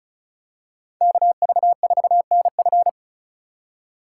Morse code, heard as a keyed tone, about 36 wpm, 700 Hz. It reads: KV4NF
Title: USA Call Signs - 3x Character Spacing - 35wpm